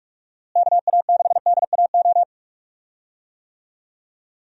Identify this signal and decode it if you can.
Morse code, heard as a keyed tone, about 46 wpm, 700 Hz. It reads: KA6DAO
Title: USA Call Signs - 6 Characters - Repeated 2x - 45wpm